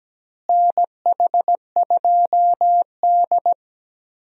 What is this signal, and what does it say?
Morse code, heard as a keyed tone, about 17 wpm, 700 Hz. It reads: NH2D
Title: USA Call Signs - 4 Characters - Repeated 3x - 17wpm